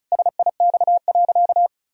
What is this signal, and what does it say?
Morse code, heard as a keyed tone, about 36 wpm, 700 Hz. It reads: SIX.
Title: Sentences from Top 1000 Words - Review Part 2 - 35wpm